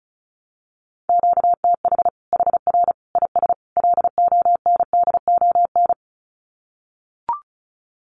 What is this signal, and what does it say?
Morse code, heard as a keyed tone, about 34 wpm, 700 Hz. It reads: QTH HR IS LONDON
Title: Common QSO Phrases 35wpm